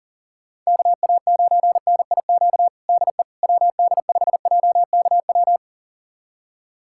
Morse code, heard as a keyed tone, about 37 wpm, 700 Hz. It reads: KA9NIQ DE WD5JKW
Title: USA Callsign DE Callsign - 6 Characters - 40wpm